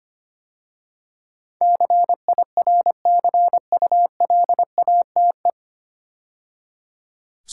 Morse code, heard as a keyed tone, about 25 wpm, 700 Hz. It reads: CIRCULATE
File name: practice / 25wpm / 9 Letter Words - First 3 Letters Given - 25wpm